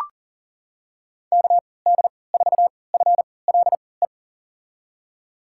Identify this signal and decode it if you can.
Morse code, heard as a keyed tone, about 40 wpm, spaced as for 23 wpm, 700 Hz. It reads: KD4FLE
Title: USA Call Signs - 6 Characters - 3x Character Spacing - 40wpm